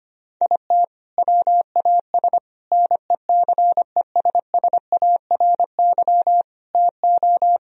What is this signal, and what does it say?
Morse code, heard as a keyed tone, about 25 wpm, 700 Hz. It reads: IT WAS NECESSARY TO
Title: Sentences from Top 1000 Words - Review Part 4 - 25wpm